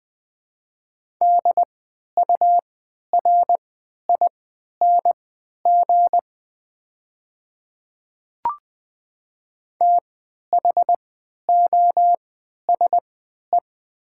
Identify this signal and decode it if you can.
Morse code, heard as a keyed tone, about 20 wpm, spaced as for 11 wpm, 700 Hz. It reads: DURING THOSE
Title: 3x Character Spacing - Top 500 Words - 20wpm